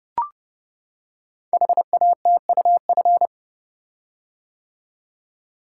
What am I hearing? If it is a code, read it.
Morse code, heard as a keyed tone, about 30 wpm, 700 Hz. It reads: HATUF
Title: Code Groups - 5 Letters - 30wpm